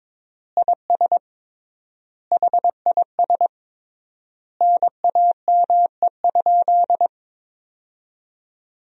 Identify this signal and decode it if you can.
Morse code, heard as a keyed tone, about 22 wpm, 700 Hz. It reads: IS HIS NAME?
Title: 3x Word-Spacing - Sentences from Top 200 Words - 22wpm